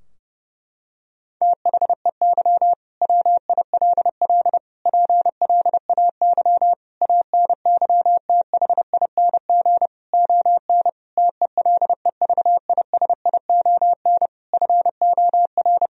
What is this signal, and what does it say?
Morse code, heard as a keyed tone, about 32 wpm, 700 Hz. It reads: THEY WILL PLAY ANYTHING ON TELEVISION FOR
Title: Sentences from Top 1000 Words - Review Part 4 - 30wpm